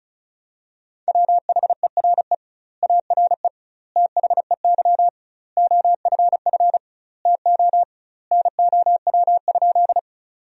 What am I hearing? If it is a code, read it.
Morse code, heard as a keyed tone, about 36 wpm, 700 Hz. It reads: WHERE ARE THEY OFF TO NOW?
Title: 2x Word-Spacing - Sentences from Top 200 Words - 35wpm